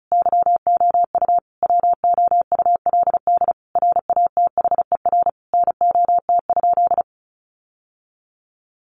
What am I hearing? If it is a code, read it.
Morse code, heard as a keyed tone, about 34 wpm, 700 Hz. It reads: YOU WOULD RATHER NOT?
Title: Sentences from Top 700 Words 35wpm